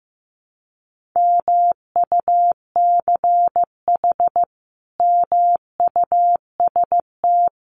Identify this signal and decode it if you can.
Morse code, heard as a keyed tone, about 15 wpm, 700 Hz. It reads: MUCH MUST
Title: Sentences from Top 1400 Words 15wpm